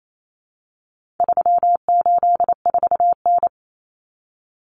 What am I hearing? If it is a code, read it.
Morse code, heard as a keyed tone, about 28 wpm, 700 Hz. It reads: 384N
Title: Common Cut Numbers - 4-Digit 28wpm